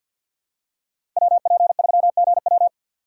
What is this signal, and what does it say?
Morse code, heard as a keyed tone, about 48 wpm, 700 Hz. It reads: WP3GW
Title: USA Call Signs - 3x Character Spacing - 50wpm